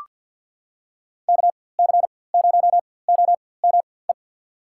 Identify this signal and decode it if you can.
Morse code, heard as a keyed tone, about 53 wpm, spaced as for 24 wpm, 700 Hz. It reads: KC0OME
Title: USA Call Signs - 6 Characters - 4x Character Spacing - 50wpm